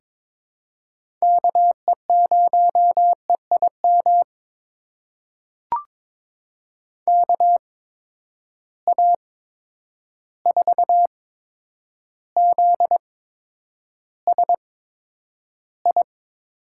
Morse code, heard as a keyed tone, about 22 wpm, spaced as for 9 wpm, 700 Hz. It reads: KE0EIM KA4ZSI
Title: USA Call Signs - 6 Characters - 8x Character Spacing - 22wpm